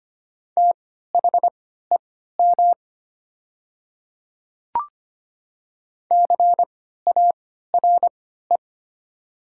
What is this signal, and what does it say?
Morse code, heard as a keyed tone, about 25 wpm, spaced as for 14 wpm, 700 Hz. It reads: THEM CARE
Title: 3x Character Spacing - Top 300 Words - 25wpm